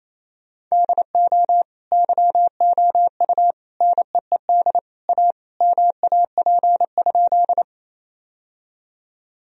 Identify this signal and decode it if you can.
Morse code, heard as a keyed tone, about 28 wpm, 700 Hz. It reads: DO YOU NEED A MAP?
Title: Sentences from Top 1000 Words - Review Part 2 - 28wpm